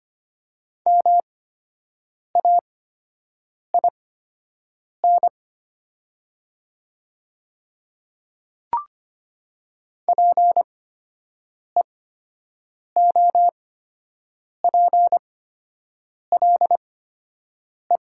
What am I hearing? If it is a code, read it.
Morse code, heard as a keyed tone, about 25 wpm, spaced as for 7 wpm, 700 Hz. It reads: MAIN PEOPLE
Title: 8x Character Spacing - Top 300 Words - 25wpm